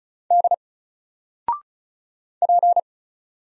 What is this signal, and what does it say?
Morse code, heard as a keyed tone, about 34 wpm, 700 Hz. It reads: D P
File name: Single-Letter-Number-35wpm